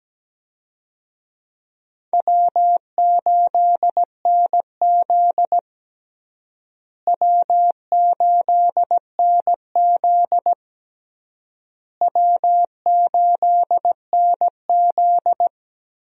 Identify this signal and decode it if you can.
Morse code, heard as a keyed tone, about 17 wpm, 700 Hz. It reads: W8NZ W8NZ W8NZ
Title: USA Call Signs - 4 Characters - Repeated 3x - 17wpm